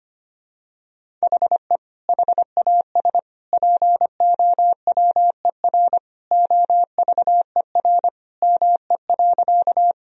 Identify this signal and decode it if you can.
Morse code, heard as a keyed tone, about 25 wpm, 700 Hz. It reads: HE HAS POWER OVER ME.
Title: Sentences from Top 1000 Words - Review Part 2 - 25wpm